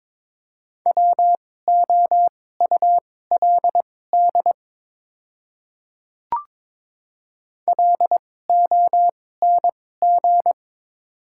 Morse code, heard as a keyed tone, about 22 wpm, spaced as for 16 wpm, 700 Hz. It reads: WOULD LONG
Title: 2x Character Spacing - Top 100 Words - 22wpm